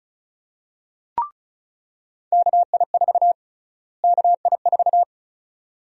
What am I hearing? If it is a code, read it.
Morse code, heard as a keyed tone, about 34 wpm, 700 Hz. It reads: KI4 KI4
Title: USA Call Sign Prefixes - Repeated 2x - 35wpm